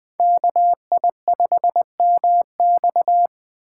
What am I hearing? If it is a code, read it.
Morse code, heard as a keyed tone, about 20 wpm, 700 Hz. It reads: KI5MX